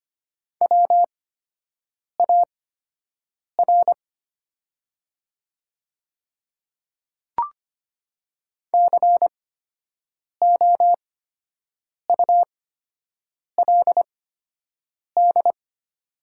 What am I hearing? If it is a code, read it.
Morse code, heard as a keyed tone, about 25 wpm, spaced as for 7 wpm, 700 Hz. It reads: WAR COULD
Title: 8x Character Spacing - Top 500 Words - 25wpm